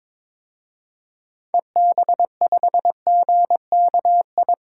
Morse code, heard as a keyed tone, about 22 wpm, 700 Hz. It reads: EB5GKI
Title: DXCC-100 Call Signs - 6 Characters - Repeated 2x - 22wpm